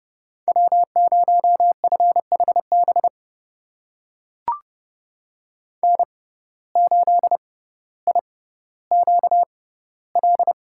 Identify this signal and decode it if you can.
Morse code, heard as a keyed tone, about 30 wpm, spaced as for 15 wpm, 700 Hz. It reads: W0FHB N8IQL
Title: USA Call Signs - 5 Characters - 6x Character Spacing - 30wpm